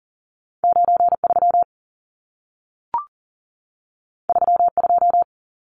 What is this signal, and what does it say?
Morse code, heard as a keyed tone, about 40 wpm, 700 Hz. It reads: 93 32
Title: Numbers - 2-Digit 40wpm